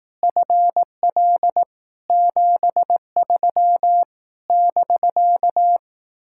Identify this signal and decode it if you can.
Morse code, heard as a keyed tone, about 18 wpm, 700 Hz. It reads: FL 73 <BK>
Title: Speed-Racing Common QSO Phrases Encore - 15wpm